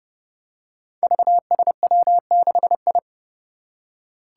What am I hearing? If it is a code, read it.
Morse code, heard as a keyed tone, about 29 wpm, 700 Hz. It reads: VSW6I
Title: Five Challenging Letter-Number Combinations 30wpm